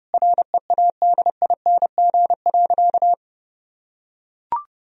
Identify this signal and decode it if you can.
Morse code, heard as a keyed tone, about 32 wpm, 700 Hz. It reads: READING.
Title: Sentences from Top 1000 Words - Review Part 3 - 30wpm